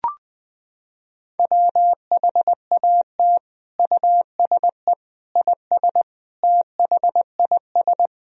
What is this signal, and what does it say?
Morse code, heard as a keyed tone, about 20 wpm, 700 Hz. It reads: WHAT USE IS THIS
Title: Sentences from Top 1000 Words - Review Part 1 - 20wpm